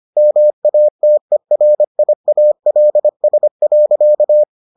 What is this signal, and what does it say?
Morse code, heard as a keyed tone, about 25 wpm, 600 Hz. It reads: MATERIALS.